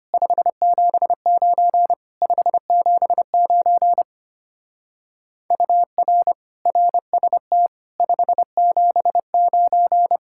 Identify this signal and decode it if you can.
Morse code, heard as a keyed tone, about 30 wpm, 700 Hz. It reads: 579 579 UR RST 579
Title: Speed-Racing Common QSO Phrases - 25wpm